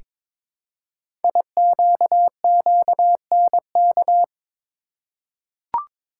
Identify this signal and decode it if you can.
Morse code, heard as a keyed tone, about 22 wpm, 700 Hz. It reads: IQQNK
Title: Code Groups - 5 Letters - 22wpm